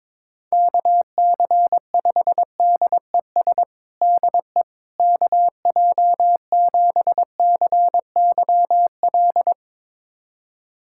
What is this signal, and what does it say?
Morse code, heard as a keyed tone, about 22 wpm, 700 Hz. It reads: KC5DES DE KJ7CYL
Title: USA Callsign DE Callsign - 6 Characters - 22wpm